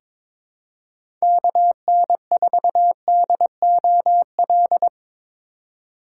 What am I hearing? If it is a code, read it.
Morse code, heard as a keyed tone, about 22 wpm, 700 Hz. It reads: KN4DOL
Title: USA Call Signs - 6 Characters - 2x Character Spacing - 22wpm